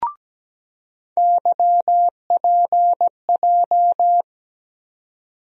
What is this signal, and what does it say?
Morse code, heard as a keyed tone, about 17 wpm, 700 Hz. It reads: YPJ
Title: USA Call Sign Postfixes - 3 Letters 17wpm